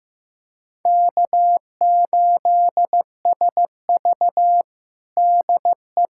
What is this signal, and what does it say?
Morse code, heard as a keyed tone, about 15 wpm, 700 Hz. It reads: K8SV DE
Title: USA Callsign DE Callsign - 4 Characters - 15wpm